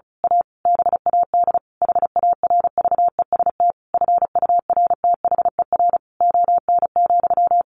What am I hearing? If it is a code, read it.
Morse code, heard as a keyed tone, about 34 wpm, 700 Hz. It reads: A BAD HARVEST FURTHER ON,
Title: A Sound Of Thunder 35wpm